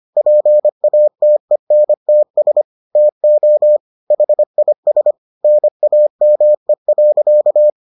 Morse code, heard as a keyed tone, about 25 wpm, 600 Hz. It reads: PATENTS TO HIS NAME.